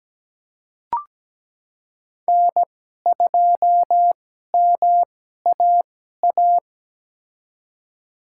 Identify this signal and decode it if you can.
Morse code, heard as a keyed tone, about 17 wpm, spaced as for 12 wpm, 700 Hz. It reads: N2MAA
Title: USA Call Signs - 2x Character Spacing - 17wpm